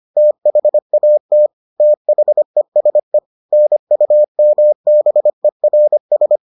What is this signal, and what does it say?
Morse code, heard as a keyed tone, about 25 wpm, 600 Hz. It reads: THAT THESE NUMBERS